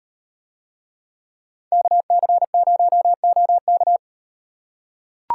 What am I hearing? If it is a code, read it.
Morse code, heard as a keyed tone, about 37 wpm, 700 Hz. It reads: KC0OK